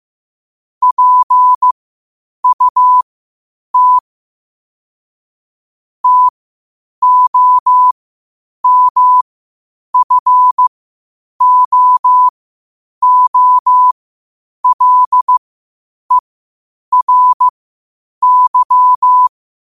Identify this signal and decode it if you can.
Morse code, heard as a keyed tone, about 15 wpm, spaced as for 8 wpm, 1000 Hz. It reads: PUT TOMFOOLERY
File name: qotdfiles / 8wpm